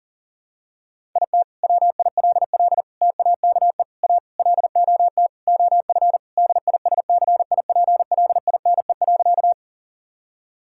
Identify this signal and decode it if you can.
Morse code, heard as a keyed tone, about 40 wpm, 690 Hz. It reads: IT WILL TAKE A LOT OF DISCIPLINE.
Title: Sentences from Top 1000 Words - Review Part 6 - 40wpm